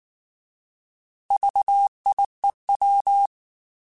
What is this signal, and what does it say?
Morse code, heard as a keyed tone, about 19 wpm, 800 Hz. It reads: VIEW